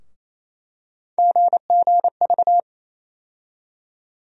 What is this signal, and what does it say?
Morse code, heard as a keyed tone, about 28 wpm, 700 Hz. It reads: GGV
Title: USA Call Sign Postfixes - 3 Letters 28wpm